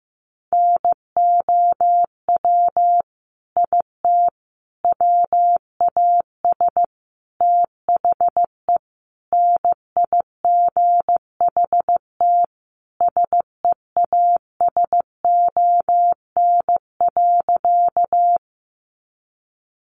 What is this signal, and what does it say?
Morse code, heard as a keyed tone, about 15 wpm, 700 Hz. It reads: NOW IT WAS THE NIGHT SEASON.
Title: Sentences from Top 1400 Words 15wpm